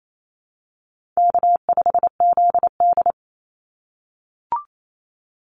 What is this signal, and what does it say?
Morse code, heard as a keyed tone, about 28 wpm, 700 Hz. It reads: K5ZD
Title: Easy Call Signs - 28wpm